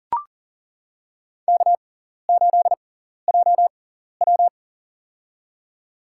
Morse code, heard as a keyed tone, about 37 wpm, spaced as for 14 wpm, 700 Hz. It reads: K8JW
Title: USA Call Signs - 4 Characters - 6x Character Spacing - 40wpm